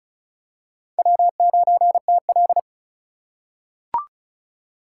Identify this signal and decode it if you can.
Morse code, heard as a keyed tone, about 36 wpm, 700 Hz. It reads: W9TL
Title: USA Call Signs - 4 Characters - 2x Character Spacing - 35wpm